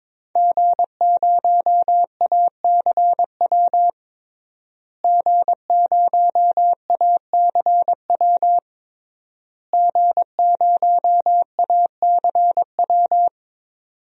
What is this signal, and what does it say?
Morse code, heard as a keyed tone, about 22 wpm, 700 Hz. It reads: G0ACW G0ACW G0ACW
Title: DXCC-100 Call Signs - 5 Characters - Repeated 3x - 22wpm